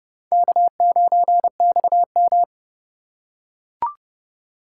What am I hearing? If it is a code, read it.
Morse code, heard as a keyed tone, about 29 wpm, 700 Hz. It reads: K9XM